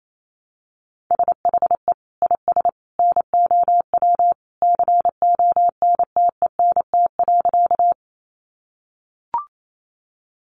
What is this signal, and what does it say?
Morse code, heard as a keyed tone, about 28 wpm, 700 Hz. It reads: SHE IS NOW CONTENT.
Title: Sentences from Top 900 Words 28wpm